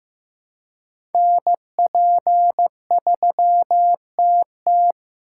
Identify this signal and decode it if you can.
Morse code, heard as a keyed tone, about 15 wpm, 700 Hz. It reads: NP3TT
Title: USA Call Signs - 8x Character Spacing - 15wpm